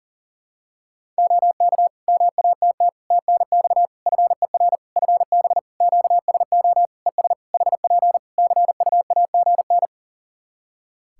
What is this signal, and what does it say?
Morse code, heard as a keyed tone, about 42 wpm, 700 Hz. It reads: OK MATT TNX FER FB QSO ES HP CUAGN
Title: Practice QSO - Exchange 4 - Ending - 40wpm